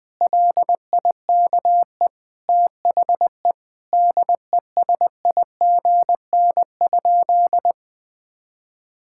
Morse code, heard as a keyed tone, about 20 wpm, 700 Hz. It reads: LIKE THE DESIGN?
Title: Sentences from Top 1000 Words - Review Part 5 - 20wpm